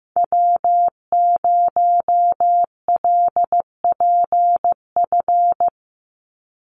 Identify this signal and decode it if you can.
Morse code, heard as a keyed tone, about 15 wpm, 700 Hz. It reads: W0LPF
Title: Call Signs Repeated 3x - 15wpm